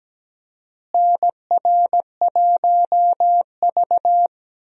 Morse code, heard as a keyed tone, about 17 wpm, 700 Hz. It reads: NR1V
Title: USA Call Signs - 4 Characters - Repeated 2x - 17wpm